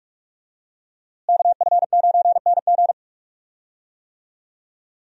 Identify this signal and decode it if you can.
Morse code, heard as a keyed tone, about 45 wpm, 700 Hz. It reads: KF9NG
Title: USA Call Signs - 5 Characters - Repeated 3x - 45wpm